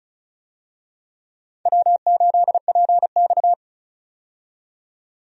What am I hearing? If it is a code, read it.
Morse code, heard as a keyed tone, about 34 wpm, 700 Hz. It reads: W8PX
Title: USA Call Signs - 4 Characters - 35wpm